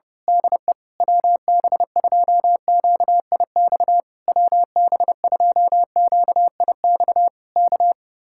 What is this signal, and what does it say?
Morse code, heard as a keyed tone, about 31 wpm, 700 Hz. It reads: DE WB2QIX WB2QIX K
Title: Speed-Racing Common QSO Phrases - 25wpm